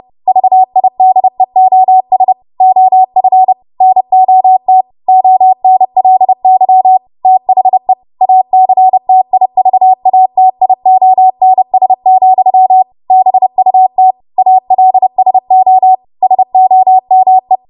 Morse code, heard as a keyed tone, about 30 wpm, 750 Hz. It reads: VIDEOS OF NOT ONLY THE ACTIVATIONS, BUT ALSO SOME
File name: Morse / 30WPM